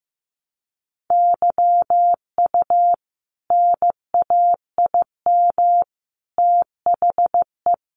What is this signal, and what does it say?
Morse code, heard as a keyed tone, about 15 wpm, 700 Hz. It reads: YU NAIM THE
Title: A Sound Of Thunder 15wpm